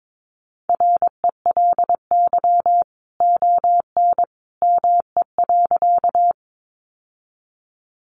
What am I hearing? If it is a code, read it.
Morse code, heard as a keyed tone, about 22 wpm, 700 Hz. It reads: RELY ON ME.
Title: Sentences from Top 1200 Words 22wpm